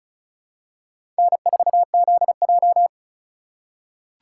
Morse code, heard as a keyed tone, about 36 wpm, 690 Hz. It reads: N4ZJ
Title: USA Call Signs - 4 Characters - 6x Character Spacing - 35wpm